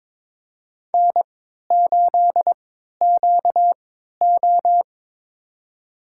Morse code, heard as a keyed tone, about 22 wpm, spaced as for 12 wpm, 700 Hz. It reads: N8QO
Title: USA Call Signs - 4 Characters - 3x Character Spacing - 22wpm